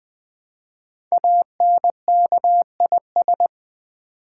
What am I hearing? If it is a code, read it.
Morse code, heard as a keyed tone, about 20 wpm, 700 Hz. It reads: ANKIS
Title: Code Groups - 5 Letters - 20wpm